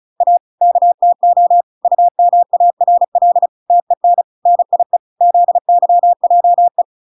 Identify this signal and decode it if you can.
Morse code, heard as a keyed tone, about 34 wpm, 700 Hz. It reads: A KTO UMARL TEN NIE ZYJE